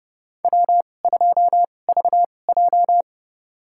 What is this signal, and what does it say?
Morse code, heard as a keyed tone, about 30 wpm, spaced as for 22 wpm, 700 Hz. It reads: W2VJ
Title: USA Call Signs - 4 Characters - 2x Character Spacing - 30wpm